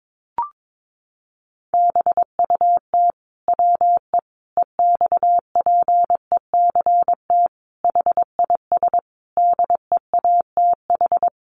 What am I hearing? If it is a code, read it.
Morse code, heard as a keyed tone, about 22 wpm, 700 Hz. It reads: BUT WE EXPECT HIS DEATH
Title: Sentences from Top 1100 Words 22wpm